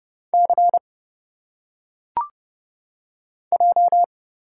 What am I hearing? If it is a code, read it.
Morse code, heard as a keyed tone, about 30 wpm, 700 Hz. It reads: C J
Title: Single Letters - Phonetic Pronunciation 30wpm